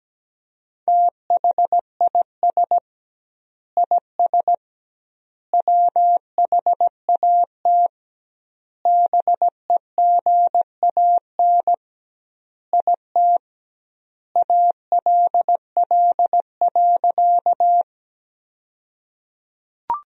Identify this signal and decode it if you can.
Morse code, heard as a keyed tone, about 17 wpm, 700 Hz. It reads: THIS IS WHAT BEGAN IT ALL.
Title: 2x Word-Spacing - Sentences from Top 300 Words - 17wpm